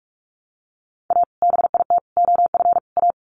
Morse code, heard as a keyed tone, about 44 wpm, 700 Hz. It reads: A BIT OF A
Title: Sets of 4 Words 45wpm